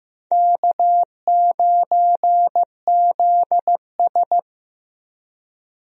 Morse code, heard as a keyed tone, about 15 wpm, 700 Hz. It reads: K9ZS
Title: USA Call Signs - 4 Characters - Repeated 3x - 15wpm